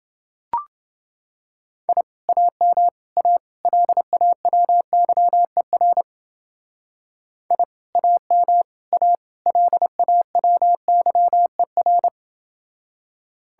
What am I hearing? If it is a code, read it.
Morse code, heard as a keyed tone, about 31 wpm, 700 Hz. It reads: I AM A LAWYER I AM A LAWYER